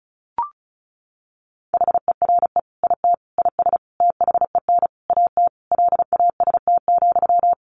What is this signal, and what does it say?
Morse code, heard as a keyed tone, about 34 wpm, 700 Hz. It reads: HERE IT IS THEN AT LAST,
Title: Sentences from Top 1500 Words 35wpm